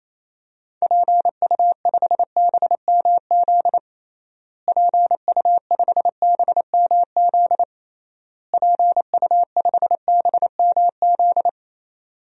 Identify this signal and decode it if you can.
Morse code, heard as a keyed tone, about 28 wpm, 700 Hz. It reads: PU5BMZ PU5BMZ PU5BMZ
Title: DXCC-100 Call Signs - 6 Characters - Repeated 3x - 28wpm